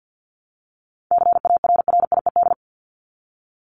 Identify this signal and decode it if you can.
Morse code, heard as a keyed tone, about 50 wpm, 700 Hz. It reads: CARRIED
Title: ED Suffix - Top 100 Words with Base Words 50wpm